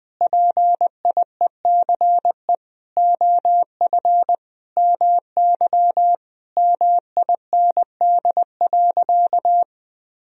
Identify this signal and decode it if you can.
Morse code, heard as a keyed tone, about 20 wpm, 700 Hz. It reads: PIECE OF MY MIND.
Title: Sentences from Top 1000 Words - Review Part 2 - 20wpm